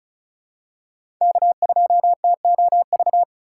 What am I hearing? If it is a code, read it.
Morse code, heard as a keyed tone, about 36 wpm, 700 Hz. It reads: K2TOV
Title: USA Call Signs - 5 Characters - 2x Character Spacing - 35wpm